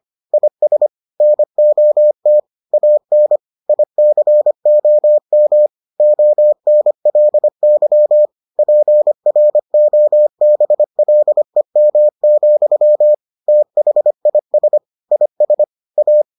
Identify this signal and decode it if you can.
Morse code, heard as a keyed tone, about 25 wpm, 600 Hz. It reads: IS NOT AN ICOM ONLY PROBLEM, THIS IS A